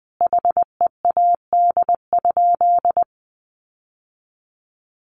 Morse code, heard as a keyed tone, about 20 wpm, 700 Hz. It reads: HEAD?
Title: Sentences from Top 1100 Words 20wpm